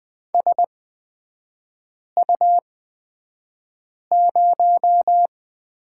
Morse code, heard as a keyed tone, about 20 wpm, spaced as for 5 wpm, 700 Hz. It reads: SU0
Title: 236 Lesson 38e - Letter q - Three Characters 20wpm